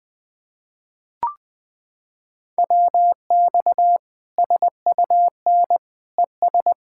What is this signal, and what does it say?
Morse code, heard as a keyed tone, about 21 wpm, 700 Hz. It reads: WX SUN ES
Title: Practice QSO - Exchange 2 - Info - 20wpm